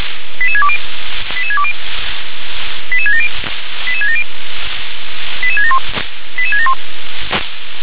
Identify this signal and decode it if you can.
Five-tone selective call (ZVEI), heard as quick runs of five tones, tones of about 70 ms. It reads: 88620 88620 88670 88670 88671 88671